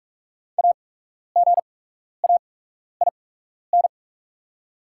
Morse code, heard as a keyed tone, about 44 wpm, spaced as for 12 wpm, 700 Hz. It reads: AGAIN
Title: 8x Character Spacing - Top 300 Words - 45wpm